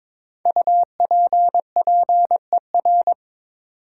Morse code, heard as a keyed tone, about 23 wpm, 700 Hz. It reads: UPPER
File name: 5 Letter Words - First 3 Letters Given - 22wpm